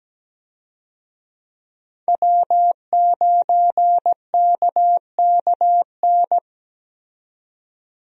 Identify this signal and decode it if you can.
Morse code, heard as a keyed tone, about 17 wpm, 700 Hz. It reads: W9KKN
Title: USA Call Signs - 5 Characters - 17wpm